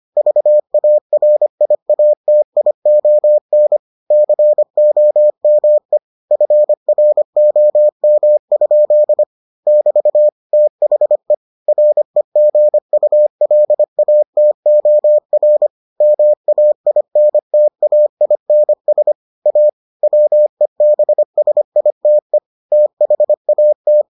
Morse code, heard as a keyed tone, about 25 wpm, 600 Hz. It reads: VARIATION COME FROM? <BT> THE REGULATOR MAINTAINS A WEBSITE THAT